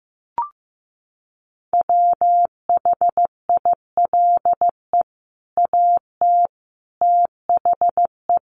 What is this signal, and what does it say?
Morse code, heard as a keyed tone, about 15 wpm, 700 Hz. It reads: WHILE AT THE
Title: Sets of 5 Words 15wpm